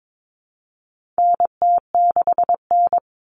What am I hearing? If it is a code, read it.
Morse code, heard as a keyed tone, about 22 wpm, 700 Hz. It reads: NT6N
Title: All Cut Numbers - 4-Digit 22wpm